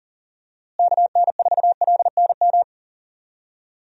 Morse code, heard as a keyed tone, about 40 wpm, 700 Hz. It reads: KN4LNM
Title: USA Call Signs - 6 Characters - Repeated 2x - 40wpm